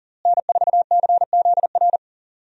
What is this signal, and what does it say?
Morse code, heard as a keyed tone, about 40 wpm, 700 Hz. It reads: N4CZR
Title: USA Call Signs - 5 Characters - Repeated 2x - 40wpm